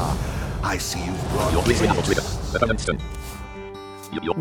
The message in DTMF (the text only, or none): none